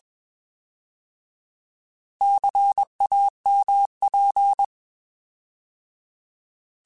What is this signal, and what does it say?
Morse code, heard as a keyed tone, about 21 wpm, 800 Hz. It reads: CAMP
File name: Alphabetical / Alphabetical mp3s / Cs @20